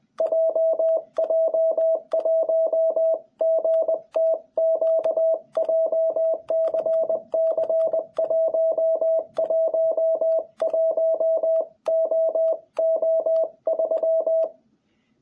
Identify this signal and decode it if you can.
Morse code, heard as a keyed tone, about 20 wpm, 660 Hz. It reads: JJ1GTQ JCC111OO3
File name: jh8cbh / 2021cw